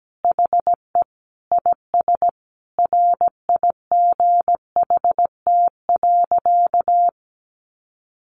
Morse code, heard as a keyed tone, about 17 wpm, 700 Hz. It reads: HE IS RIGHT.
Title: Sentences from Top 1300 Words 17wpm